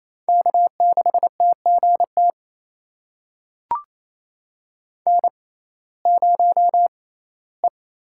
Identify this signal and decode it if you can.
Morse code, heard as a keyed tone, about 28 wpm, spaced as for 17 wpm, 700 Hz. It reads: K6TGT N0E
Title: USA Call Signs - 5 Characters - 6x Character Spacing - 28wpm